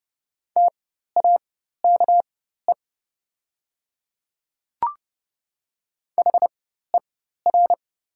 Morse code, heard as a keyed tone, about 30 wpm, spaced as for 14 wpm, 700 Hz. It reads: TAKE HER
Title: 4x Character Spacing - Top 200 Words - 30wpm